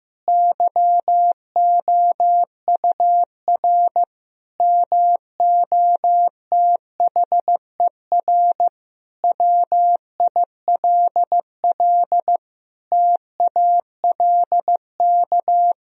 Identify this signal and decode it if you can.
Morse code, heard as a keyed tone, about 16 wpm, 700 Hz. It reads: YOUR MOTHER WILL TALK
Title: Sentences from Top 1000 Words - Review Part 1 - 15wpm